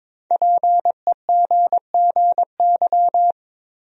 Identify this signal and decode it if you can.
Morse code, heard as a keyed tone, about 22 wpm, 700 Hz. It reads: PEGGY
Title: USA Common Names - 22wpm